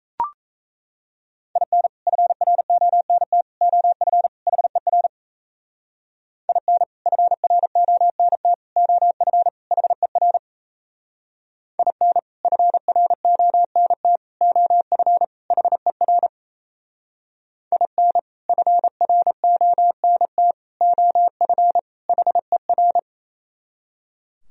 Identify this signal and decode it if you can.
Morse code, heard as a keyed tone, about 36 wpm, 700 Hz. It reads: IN FRONT OF HER IN FRONT OF HER IN FRONT OF HER IN FRONT OF HER